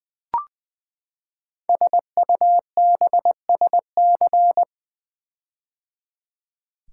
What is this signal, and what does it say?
Morse code, heard as a keyed tone, about 20 wpm, 700 Hz. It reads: SUBSC
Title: Build A Word - 5 Letters At A Time - 12-Letter Words - 20wpm